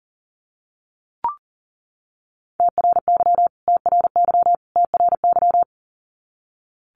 Morse code, heard as a keyed tone, about 40 wpm, 700 Hz. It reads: TRY TRY TRY
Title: Top 500 Words Repeated 40wpm